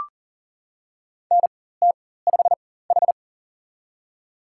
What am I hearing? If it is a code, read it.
Morse code, heard as a keyed tone, about 40 wpm, spaced as for 19 wpm, 700 Hz. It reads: NT5H
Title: USA Call Signs - 4x Character Spacing - 40wpm